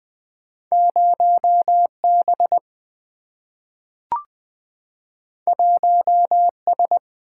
Morse code, heard as a keyed tone, about 20 wpm, 700 Hz. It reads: 0B 1S
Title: Single-Letter Single-Number 20wpm